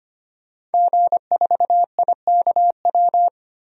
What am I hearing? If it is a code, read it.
Morse code, heard as a keyed tone, about 25 wpm, 700 Hz. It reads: G4IKW
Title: DXCC-100 Call Signs - 5 Characters - 25wpm